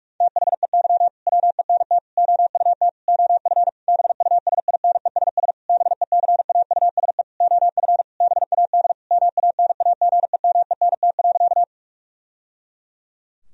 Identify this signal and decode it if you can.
Morse code, heard as a keyed tone, about 46 wpm, 700 Hz. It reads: THEY WENT OUT OF BUSINESS BECAUSE OF BAD MANAGEMENT.
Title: Sentences from Top 1000 Words - Review Part 4 - 45wpm